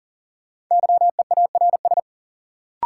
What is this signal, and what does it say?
Morse code, heard as a keyed tone, about 40 wpm, 700 Hz. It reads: YEARS
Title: Sentences from Top 200 Words - New Word Drill - 40wpm